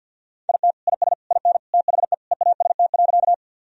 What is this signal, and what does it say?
Morse code, heard as a keyed tone, about 50 wpm, 700 Hz. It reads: IT IS IN THE EAST.
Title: Sentences from Top 1000 Words - Review Part 3 - 50wpm